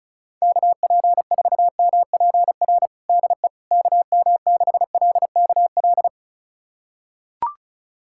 Morse code, heard as a keyed tone, about 36 wpm, 700 Hz. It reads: KP4MPR DE KM6LKL
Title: USA Callsign DE Callsign - 6 Characters - 35wpm